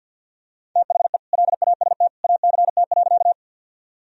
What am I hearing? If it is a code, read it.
Morse code, heard as a keyed tone, about 53 wpm, 700 Hz. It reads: THE LAST ACT.
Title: Sentences from Top 1000 Words - Review Part 1 - 50wpm